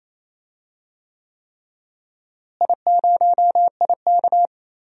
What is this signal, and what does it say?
Morse code, heard as a keyed tone, about 27 wpm, 700 Hz. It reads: I0IK
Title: DXCC-100 Call Signs - 4 Characters - Repeated 3x - 28wpm